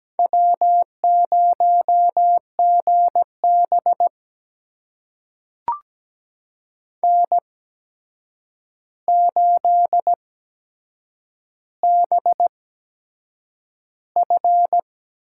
Morse code, heard as a keyed tone, about 17 wpm, spaced as for 7 wpm, 700 Hz. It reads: W0GB N8BF
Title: USA Call Signs - 4 Characters - 8x Character Spacing - 17wpm